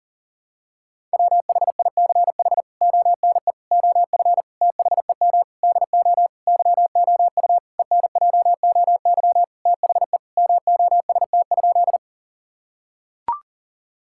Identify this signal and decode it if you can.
Morse code, heard as a keyed tone, about 40 wpm, 700 Hz. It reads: WHICH ONE OF THEM DO YOU ENJOY THE MOST?
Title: Sentences from Top 1000 Words - Review Part 5 - 40wpm